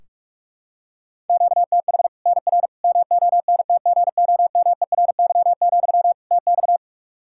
Morse code, heard as a keyed tone, about 41 wpm, 700 Hz. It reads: QTH NR MONTGOMERY, TX
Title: Practice QSO - Exchange 1 - Beginning - 45wpm